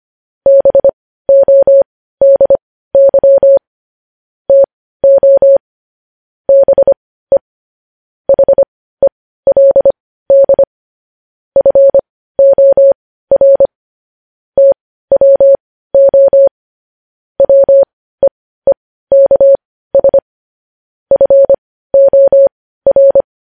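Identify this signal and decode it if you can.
Morse code, heard as a keyed tone, about 25 wpm, spaced as for 15 wpm, 550 Hz. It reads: BODY TO BE HELD FOR TWO WEEKS FOR
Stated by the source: Morse code news headlines at 15 wpm.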